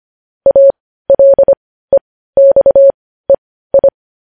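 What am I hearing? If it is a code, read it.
Morse code, heard as a keyed tone, about 25 wpm, spaced as for 15 wpm, 550 Hz. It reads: ALEXEI
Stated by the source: Morse code news headlines practice at 15, preceded by vvv.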